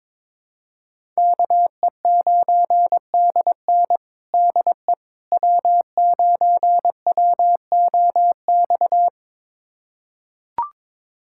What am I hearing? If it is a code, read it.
Morse code, heard as a keyed tone, about 22 wpm, 700 Hz. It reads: KE9DN DE W9WOX
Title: USA Callsign DE Callsign - 5 Characters - 22wpm